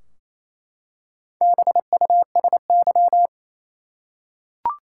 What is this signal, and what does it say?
Morse code, heard as a keyed tone, about 28 wpm, 700 Hz. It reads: BUSY
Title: Four Letter Words - 28wpm